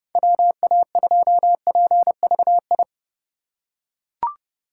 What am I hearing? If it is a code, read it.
Morse code, heard as a keyed tone, about 30 wpm, 700 Hz. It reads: WA2PVI
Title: USA Call Signs - 6 Characters - 2x Character Spacing - 30wpm